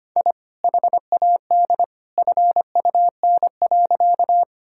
Morse code, heard as a keyed tone, about 25 wpm, 700 Hz. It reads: I HAD FUN.
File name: practice / 25wpm / Sentences with 2-Word Phrases - 25wpm